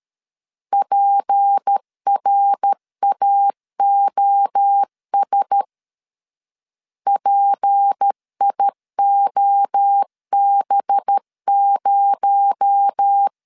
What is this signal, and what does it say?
Morse code, heard as a keyed tone, about 13 wpm, 800 Hz. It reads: PRAOS PIOB0
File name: CW / DET / 1 6 1500C S A 12wpm 3440